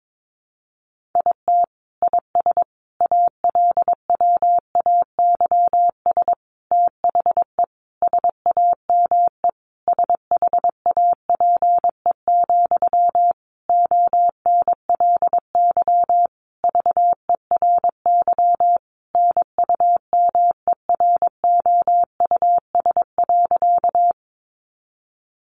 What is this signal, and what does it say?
Morse code, heard as a keyed tone, about 22 wpm, 700 Hz. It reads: IT IS ALWAYS THE SAME SHAPE, ONLY VERY NUMEROUS.
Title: Sentences from Top 1400 Words 22wpm